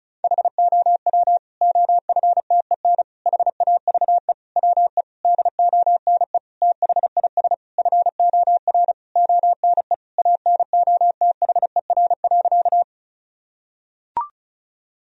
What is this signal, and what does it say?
Morse code, heard as a keyed tone, about 34 wpm, 700 Hz. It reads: HOW OFTEN HAVE WE DONE THIS FOR ONE ANOTHER.
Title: Sentences from Top 1000 Words - Review Part 1 - 35wpm